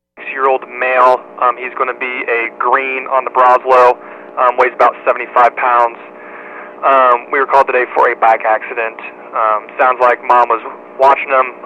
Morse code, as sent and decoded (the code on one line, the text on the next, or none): none